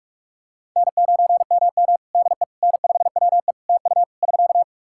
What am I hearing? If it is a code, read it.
Morse code, heard as a keyed tone, about 46 wpm, 700 Hz. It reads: N9MM DE N5WE TU <SK>
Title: Practice QSO - Exchange 4 - Ending - 45wpm